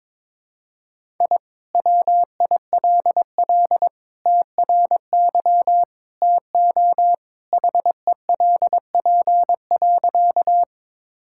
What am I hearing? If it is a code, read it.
Morse code, heard as a keyed tone, about 21 wpm, 700 Hz. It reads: I WILL TRY TO HELP.
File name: practice / 22wpm / Sentences with 3-Word Phrases - 22wpm